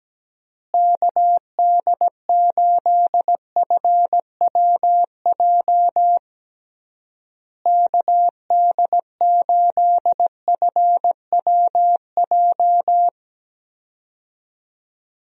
Morse code, heard as a keyed tone, about 17 wpm, 700 Hz. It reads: KD8FWJ KD8FWJ
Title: USA Call Signs - 6 Characters - Repeated 3x - 17wpm